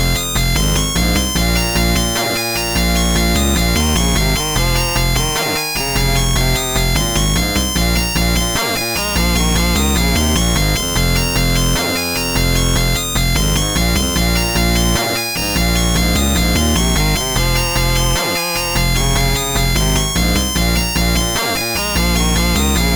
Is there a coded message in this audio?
No.